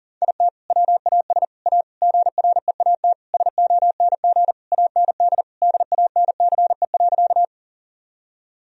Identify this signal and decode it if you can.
Morse code, heard as a keyed tone, about 40 wpm, 700 Hz. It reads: IT WAS A GREAT SONG AND DANCE.
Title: Sentences from Top 1000 Words - Review Part 4 - 40wpm